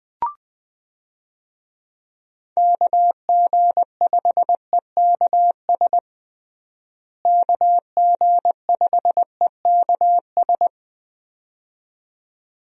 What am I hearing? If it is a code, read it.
Morse code, heard as a keyed tone, about 20 wpm, 700 Hz. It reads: KG5EKS KG5EKS
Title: USA Call Signs - 6 Characters - Repeated 2x - 20wpm